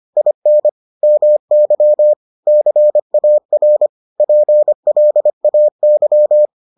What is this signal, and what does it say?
Morse code, heard as a keyed tone, about 25 wpm, 600 Hz. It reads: IN MY CAR PLAY